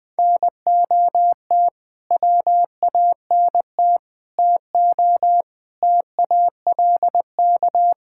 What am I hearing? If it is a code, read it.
Morse code, heard as a keyed tone, about 20 wpm, 700 Hz. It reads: NOT WANT TO TALK
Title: Build A Phrase - 5-Word Phrases - 20wpm